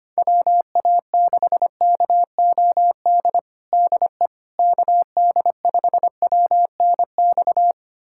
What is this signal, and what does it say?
Morse code, heard as a keyed tone, about 25 wpm, 700 Hz. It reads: WA6KOD DE KD5WNX
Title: USA Callsign DE Callsign - 6 Characters - 25wpm